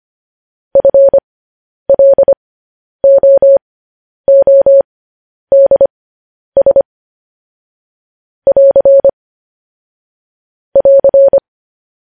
Morse code, heard as a keyed tone, about 25 wpm, spaced as for 10 wpm, 550 Hz. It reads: FLOODS <AR> <AR>